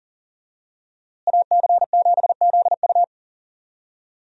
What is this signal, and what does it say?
Morse code, heard as a keyed tone, about 38 wpm, 700 Hz. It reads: AC7ZU
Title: USA Call Signs - 5 Characters - Repeated 3x - 40wpm